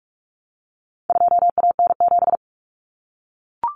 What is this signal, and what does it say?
Morse code, heard as a keyed tone, about 44 wpm, 700 Hz. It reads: 2AN7
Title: Common Cut Numbers - 4-Digit 45wpm